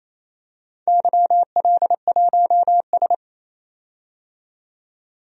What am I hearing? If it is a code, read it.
Morse code, heard as a keyed tone, about 28 wpm, 700 Hz. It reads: YL1S
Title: DXCC-100 Call Signs - 4 Characters - 28wpm